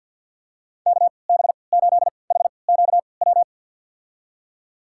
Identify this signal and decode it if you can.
Morse code, heard as a keyed tone, about 52 wpm, spaced as for 29 wpm, 700 Hz. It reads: KB8HQW
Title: USA Call Signs - 6 Characters - 3x Character Spacing - 50wpm